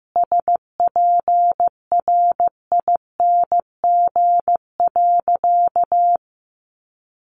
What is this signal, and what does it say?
Morse code, heard as a keyed tone, about 15 wpm, 700 Hz. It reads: SPRING.
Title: Sentences from Top 1200 Words 15wpm